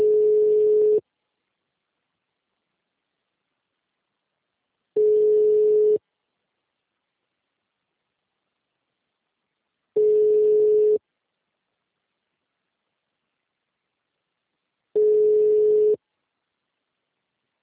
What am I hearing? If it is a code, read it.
Ringback tone (Europe, 425 Hz).
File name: out- -2004- - - .53308